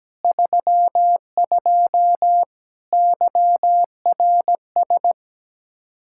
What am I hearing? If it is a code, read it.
Morse code, heard as a keyed tone, about 17 wpm, 700 Hz. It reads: 32 YRS
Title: Practice QSO - Exchange 3 - More Info - 17wpm